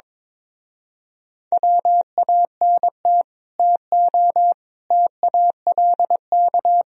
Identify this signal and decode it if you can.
Morse code, heard as a keyed tone, about 22 wpm, 700 Hz. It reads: WANT TO TALK
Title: Speed-Racing - Sets of 4 Words 22wpm